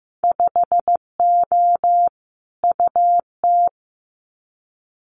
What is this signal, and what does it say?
Morse code, heard as a keyed tone, about 15 wpm, 700 Hz. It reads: 5O UT
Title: Winter Field Day Exchanges 15wpm